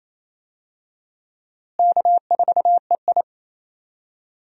Morse code, heard as a keyed tone, about 28 wpm, 700 Hz. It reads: K4EI
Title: USA Call Signs - 4 Characters - Repeated 3x - 28wpm